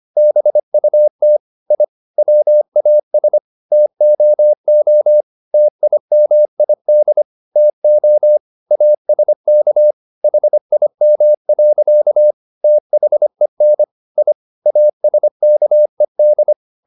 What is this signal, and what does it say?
Morse code, heard as a keyed tone, about 25 wpm, 600 Hz. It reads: BUT I WAS TOO TIMID TO ASK HIM. THEN I ASKED
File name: the old man and the sea part 1 25wpm